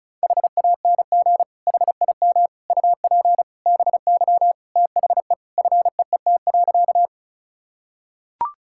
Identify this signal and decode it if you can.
Morse code, heard as a keyed tone, about 36 wpm, 700 Hz. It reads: HANG HIM UP BY THE FEET.
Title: Sentences from Top 1600 Words 35wpm